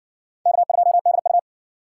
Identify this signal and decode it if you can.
Morse code, heard as a keyed tone, about 58 wpm, 700 Hz. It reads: K2DU